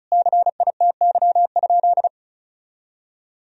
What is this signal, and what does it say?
Morse code, heard as a keyed tone, about 36 wpm, 700 Hz. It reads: CITY?
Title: Sentences from Top 1000 Words - Review Part 1 - 35wpm